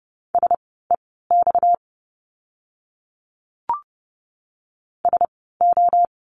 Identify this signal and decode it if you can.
Morse code, heard as a keyed tone, about 30 wpm, spaced as for 17 wpm, 700 Hz. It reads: SEX SO
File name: course / 30:10wpm / 230 Lesson 37f - Letter x - Words 30wpm